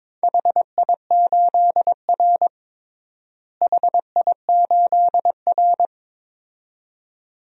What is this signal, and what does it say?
Morse code, heard as a keyed tone, about 22 wpm, 700 Hz. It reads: HI8R HI8R
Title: DXCC-100 Call Signs - 4 Characters - Repeated 2x - 22wpm